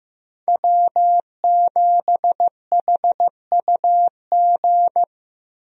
Morse code, heard as a keyed tone, about 16 wpm, 700 Hz. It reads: W7HUG
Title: USA Call Signs - 5 Characters - Repeated 2x - 15wpm